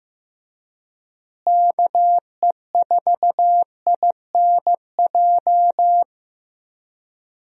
Morse code, heard as a keyed tone, about 15 wpm, 700 Hz. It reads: KE4INJ
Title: USA Call Signs - 6 Characters - Repeated 3x - 15wpm